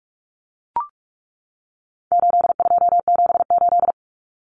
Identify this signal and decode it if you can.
Morse code, heard as a keyed tone, about 44 wpm, 700 Hz. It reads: 8278
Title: Numbers - 4-Digit 45wpm